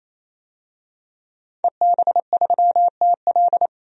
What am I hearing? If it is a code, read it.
Morse code, heard as a keyed tone, about 28 wpm, 700 Hz. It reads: EB3TL
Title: DXCC-100 Call Signs - 5 Characters - 28wpm